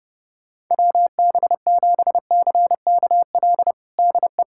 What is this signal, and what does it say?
Morse code, heard as a keyed tone, about 29 wpm, 700 Hz. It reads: WB7CKL DE
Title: USA Callsign DE Callsign - 6 Characters - 30wpm